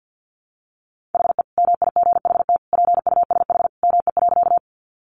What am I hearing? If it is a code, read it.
Morse code, heard as a keyed tone, about 50 wpm, 700 Hz. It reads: HE MIGHT PUSH ME.
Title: Sentences from Top 1300 Words 50wpm